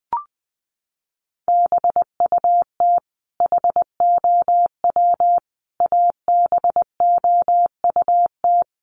Morse code, heard as a keyed tone, about 20 wpm, 700 Hz. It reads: BUT HOW ABOUT
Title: Sentences from Top 100 Words 20wpm